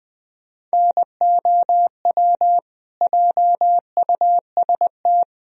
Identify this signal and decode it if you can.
Morse code, heard as a keyed tone, about 20 wpm, 700 Hz. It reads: NOW JUST
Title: Sentences from Top 1000 Words - Review Part 4 - 20wpm